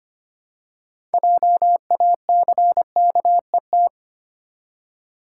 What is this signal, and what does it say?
Morse code, heard as a keyed tone, about 25 wpm, 700 Hz. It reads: JACKET
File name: practice / 25wpm / Easy Words with Hints - 25wpm